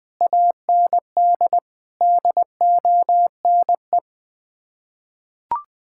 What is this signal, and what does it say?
Morse code, heard as a keyed tone, about 20 wpm, 700 Hz. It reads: AND DONE
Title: Sets of 6 Words - 20wpm